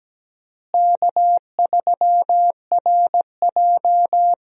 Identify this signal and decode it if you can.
Morse code, heard as a keyed tone, about 17 wpm, 700 Hz. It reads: K3RJ
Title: USA Call Signs - 4 Characters - Repeated 3x - 17wpm